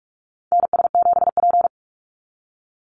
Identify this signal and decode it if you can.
Morse code, heard as a keyed tone, about 44 wpm, 700 Hz. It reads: NS7P
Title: NA SOTA Chaser Callsigns 45wpm